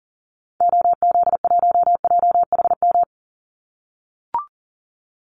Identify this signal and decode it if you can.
Morse code, heard as a keyed tone, about 40 wpm, 700 Hz. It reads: OZ1JHM
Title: Call Signs - 40wpm